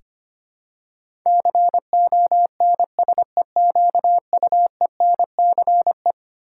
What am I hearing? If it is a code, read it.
Morse code, heard as a keyed tone, about 25 wpm, 700 Hz. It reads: CONSEQUENCE
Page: morse-code-ninja-practice-25wpm